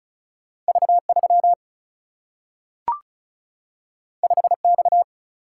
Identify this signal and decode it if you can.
Morse code, heard as a keyed tone, about 36 wpm, 700 Hz. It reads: V3 5X
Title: Single-Letter Single-Number 35wpm